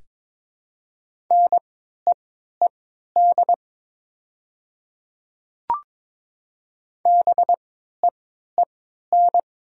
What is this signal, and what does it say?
Morse code, heard as a keyed tone, about 22 wpm, spaced as for 13 wpm, 700 Hz. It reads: NEED BEEN
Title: 3x Character Spacing - Top 200 Words - 22wpm